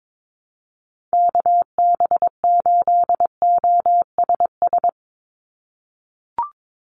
Morse code, heard as a keyed tone, about 22 wpm, 700 Hz. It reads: KB8OSS
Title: USA Call Signs 22wpm